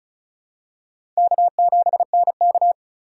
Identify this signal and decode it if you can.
Morse code, heard as a keyed tone, about 34 wpm, 700 Hz. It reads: K7NK
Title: USA Call Signs - 4 Characters - 2x Character Spacing - 35wpm